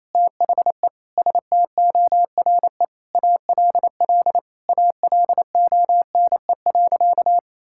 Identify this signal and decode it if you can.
Morse code, heard as a keyed tone, about 28 wpm, 700 Hz. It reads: THE STORE ALL ALONE.
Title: Sentences from Top 1000 Words - Review Part 5 - 28wpm